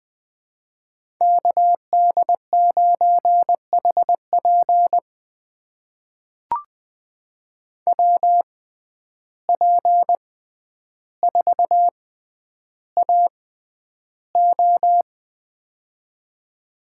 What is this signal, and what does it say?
Morse code, heard as a keyed tone, about 20 wpm, spaced as for 10 wpm, 700 Hz. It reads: KD9HP WP4AO
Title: USA Call Signs - 5 Characters - 6x Character Spacing - 20wpm